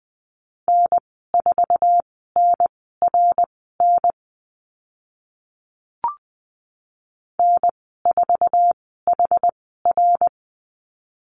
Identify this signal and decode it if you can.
Morse code, heard as a keyed tone, about 20 wpm, spaced as for 14 wpm, 700 Hz. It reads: N4NRN N4HR
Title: 062 Lesson 11g - Letter h - Call Signs 20wpm